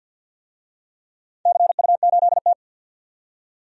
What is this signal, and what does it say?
Morse code, heard as a keyed tone, about 47 wpm, 700 Hz. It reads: CU8T
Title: DXCC-100 Call Signs - 4 Characters - 50wpm